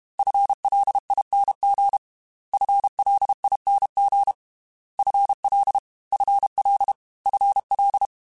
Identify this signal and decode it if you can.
Morse code, heard as a keyed tone, about 32 wpm, 800 Hz. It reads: FLING FLING FL FL FL
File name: FL / FL mp3s / FL Prefixes 3x2 Twice @30